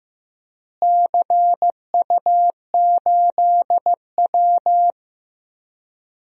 Morse code, heard as a keyed tone, about 15 wpm, 700 Hz. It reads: CU8W
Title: DXCC-100 Call Signs - 4 Characters - Repeated 3x - 15wpm